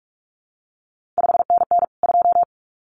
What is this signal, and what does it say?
Morse code, heard as a keyed tone, about 46 wpm, 700 Hz. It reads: 5NN 2
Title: CQ WW DX Contest Exchange Encore - 45wpm